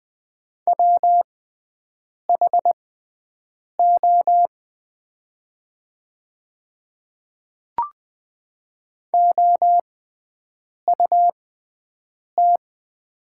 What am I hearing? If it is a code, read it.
Morse code, heard as a keyed tone, about 20 wpm, spaced as for 7 wpm, 700 Hz. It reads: WHO OUT
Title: 6x Character Spacing - Top 300 Words - 20wpm